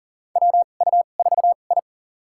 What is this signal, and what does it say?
Morse code, heard as a keyed tone, about 40 wpm, spaced as for 29 wpm, 700 Hz. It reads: WU4I
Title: USA Call Signs - 4 Characters - 2x Character Spacing - 40wpm